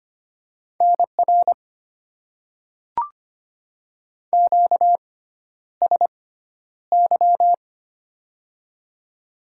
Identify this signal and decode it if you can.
Morse code, heard as a keyed tone, about 25 wpm, spaced as for 11 wpm, 700 Hz. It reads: NR QSY
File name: Top 90 QSO Elements - 6x Character Spacing - 25wpm